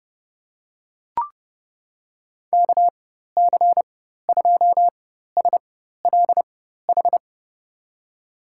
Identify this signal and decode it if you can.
Morse code, heard as a keyed tone, about 30 wpm, spaced as for 14 wpm, 700 Hz. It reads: KC2SLH
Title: USA Call Signs - 6 Characters - 4x Character Spacing - 30wpm